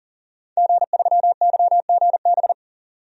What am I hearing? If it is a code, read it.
Morse code, heard as a keyed tone, about 40 wpm, 700 Hz. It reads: G3YGB
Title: DXCC-100 Call Signs - 5 Characters - 40wpm